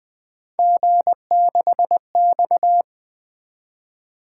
Morse code, heard as a keyed tone, about 20 wpm, 700 Hz. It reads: G6X
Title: Three Letter-Number Combinations - 20wpm